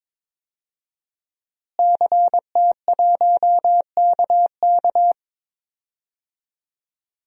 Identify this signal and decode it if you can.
Morse code, heard as a keyed tone, about 22 wpm, 700 Hz. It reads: CT1KK
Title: DXCC-100 Call Signs - 5 Characters - Repeated 3x - 22wpm